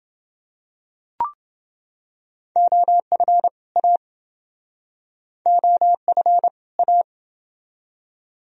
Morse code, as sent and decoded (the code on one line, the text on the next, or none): --- ..-. / .- / --- ..-. / .-
OF A OF A